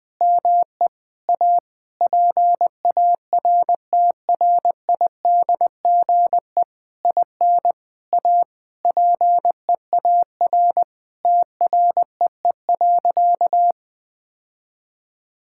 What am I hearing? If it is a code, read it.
Morse code, heard as a keyed tone, about 21 wpm, 700 Hz. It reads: ME A PARTRIDGE IN A PEAR TREE.
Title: Sentences with 9-Word Phrases - Spoken Beginning - 20wpm